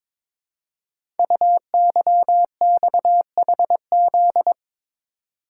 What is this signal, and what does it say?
Morse code, heard as a keyed tone, about 22 wpm, 700 Hz. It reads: UYXHZ
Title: Code Groups - 5 Letters - 22wpm